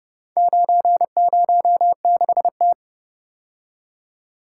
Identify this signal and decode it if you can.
Morse code, heard as a keyed tone, about 30 wpm, 700 Hz. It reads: 906T